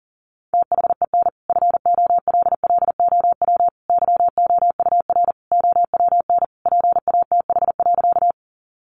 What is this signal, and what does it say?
Morse code, heard as a keyed tone, about 40 wpm, 700 Hz. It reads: THEN FOLLOW YOUR OWN PATH.
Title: Sentences from Top 1400 Words 40wpm